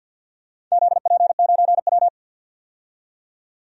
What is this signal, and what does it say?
Morse code, heard as a keyed tone, about 51 wpm, 700 Hz. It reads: ZP9W
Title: DXCC-100 Call Signs - 4 Characters - 50wpm